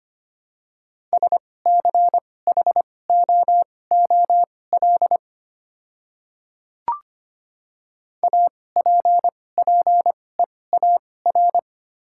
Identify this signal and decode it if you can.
Morse code, heard as a keyed tone, about 25 wpm, spaced as for 18 wpm, 700 Hz. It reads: SCHOOL APPEAR
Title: 2x Character Spacing - Top 500 Words - 25wpm